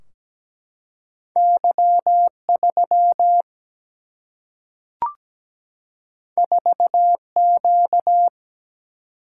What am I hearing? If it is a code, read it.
Morse code, heard as a keyed tone, about 17 wpm, 700 Hz. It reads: Y3 4Q
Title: Single-Letter Single-Number 17wpm